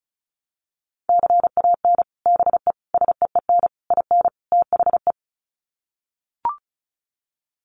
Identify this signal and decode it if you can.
Morse code, heard as a keyed tone, about 34 wpm, 700 Hz. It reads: CAN BE SEEN IN THE
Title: Sets of 5 Words Encore2 35wpm